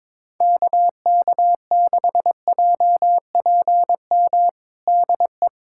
Morse code, heard as a keyed tone, about 22 wpm, 700 Hz. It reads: KK6JPM DE
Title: USA Callsign DE Callsign - 6 Characters - 22wpm